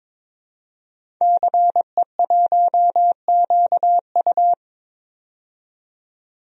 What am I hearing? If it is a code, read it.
Morse code, heard as a keyed tone, about 22 wpm, 700 Hz. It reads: CE1QU
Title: DXCC-100 Call Signs - 5 Characters - 22wpm